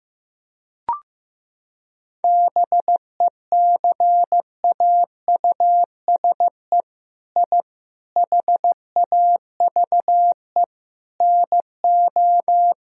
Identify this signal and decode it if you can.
Morse code, heard as a keyed tone, about 16 wpm, 700 Hz. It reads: BECAUSE I HAVE NO
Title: Sentences from Top 1000 Words - Review Part 3 - 15wpm